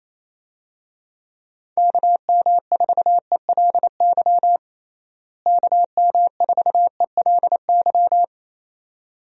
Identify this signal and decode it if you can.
Morse code, heard as a keyed tone, about 28 wpm, 700 Hz. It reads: KM4ELY KM4ELY
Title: USA Call Signs - 6 Characters - Repeated 2x - 28wpm